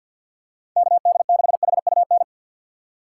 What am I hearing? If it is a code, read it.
Morse code, heard as a keyed tone, about 51 wpm, 700 Hz. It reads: K D 6 H U N
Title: USA Call Signs - 6 Characters - 6x Character Spacing - 50wpm